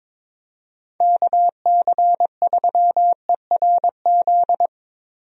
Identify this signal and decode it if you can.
Morse code, heard as a keyed tone, about 22 wpm, 700 Hz. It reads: KC3ERZ
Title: USA Call Signs - 6 Characters - Repeated 3x - 22wpm